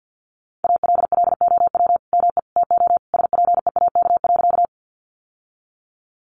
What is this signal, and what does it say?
Morse code, heard as a keyed tone, about 50 wpm, 700 Hz. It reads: ALLOW ME TO SPEAK.
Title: Sentences from Top 1100 Words 50wpm